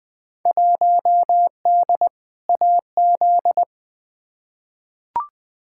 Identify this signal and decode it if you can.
Morse code, heard as a keyed tone, about 20 wpm, 700 Hz. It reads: 1D AZ
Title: Field Day Exchanges - 20wpm